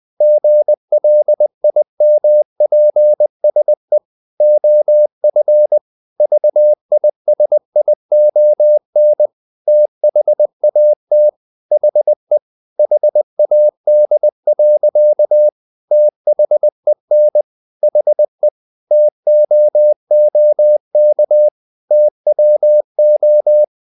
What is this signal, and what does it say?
Morse code, heard as a keyed tone, about 20 wpm, 600 Hz. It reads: GLIMPSE OF VISION THAT HE HAD. THEN HE TOOK TWO
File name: the old man and the sea part 2 20wpm